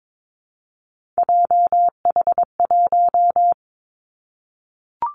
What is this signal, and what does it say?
Morse code, heard as a keyed tone, about 22 wpm, 700 Hz. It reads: JH1
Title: Callsign Prefixes for Top 25 DXCC - 22wpm